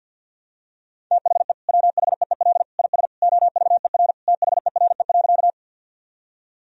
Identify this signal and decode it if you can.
Morse code, heard as a keyed tone, about 52 wpm, 700 Hz. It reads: THE WHEEL IS OVER THERE.
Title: Sentences from Top 1000 Words - Review Part 3 - 50wpm